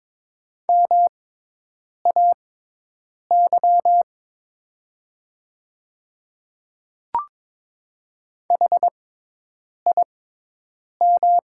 Morse code, heard as a keyed tone, about 22 wpm, spaced as for 8 wpm, 700 Hz. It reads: MAY HIM
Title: 6x Character Spacing - Top 100 Words - 22wpm